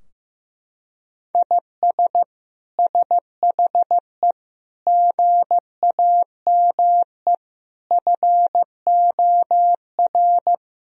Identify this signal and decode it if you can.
Morse code, heard as a keyed tone, about 16 wpm, 700 Hz. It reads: IS SHE GAME FOR
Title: Sentences from Top 1000 Words - Review Part 3 - 15wpm